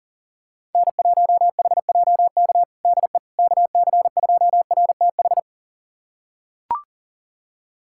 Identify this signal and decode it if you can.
Morse code, heard as a keyed tone, about 40 wpm, 700 Hz. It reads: N1HJK DE KC2RTH
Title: USA Callsign DE Callsign 40wpm